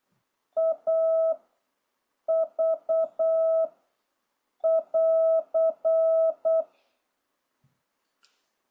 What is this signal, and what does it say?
Morse code, heard as a keyed tone, about 8 wpm, 660 Hz. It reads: A V <AR>